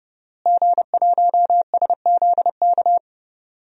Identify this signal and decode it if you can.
Morse code, heard as a keyed tone, about 30 wpm, 700 Hz. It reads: G1SZK
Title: DXCC-100 Call Signs - 5 Characters - Repeated 2x - 30wpm